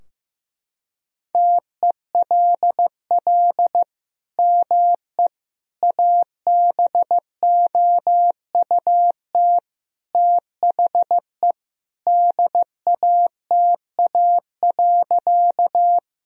Morse code, heard as a keyed tone, about 16 wpm, 700 Hz. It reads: TELL ME ABOUT THE DATA.
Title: Sentences from Top 1000 Words - Review Part 3 - 15wpm